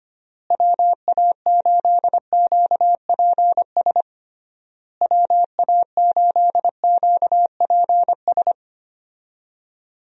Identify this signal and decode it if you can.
Morse code, heard as a keyed tone, about 25 wpm, 700 Hz. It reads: WA8QPS WA8QPS
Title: USA Call Signs - 6 Characters - Repeated 3x - 25wpm